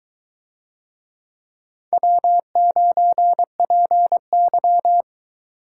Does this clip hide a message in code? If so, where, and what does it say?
Yes: Morse code from 1.9 to 5.0 s, heard as a keyed tone, about 22 wpm, 700 Hz. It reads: W9PY